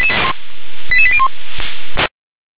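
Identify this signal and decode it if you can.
Five-tone selective call (ZVEI), heard as quick runs of five tones, tones of about 70 ms. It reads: 88801
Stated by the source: Five-tone ZVEI selective call received on a software-defined radio.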